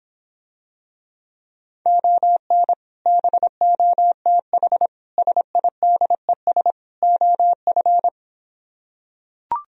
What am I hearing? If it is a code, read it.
Morse code, heard as a keyed tone, about 26 wpm, 700 Hz. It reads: ON BOTH SIDES OF